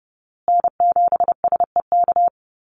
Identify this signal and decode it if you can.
Morse code, heard as a keyed tone, about 30 wpm, 700 Hz. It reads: N7SEK
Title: USA Call Signs Repeated 2x 30wpm